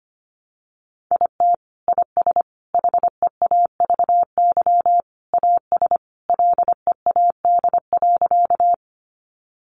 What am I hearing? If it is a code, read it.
Morse code, heard as a keyed tone, about 25 wpm, 700 Hz. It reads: IT IS HEAVY AS LEAD.
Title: Sentences from Top 1100 Words 25wpm